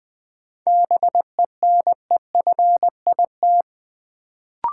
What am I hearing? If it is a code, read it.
Morse code, heard as a keyed tone, about 21 wpm, 700 Hz. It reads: BENEFIT
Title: Sentences from Top 1000 Words - Review Part 6 - New Word Drill - 20wpm